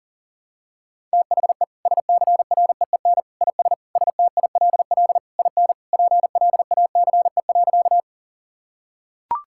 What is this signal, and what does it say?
Morse code, heard as a keyed tone, about 40 wpm, 700 Hz. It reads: THE SCREEN IS STILL IN PLACE.
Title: Sentences from Top 1000 Words - Review Part 6 - 40wpm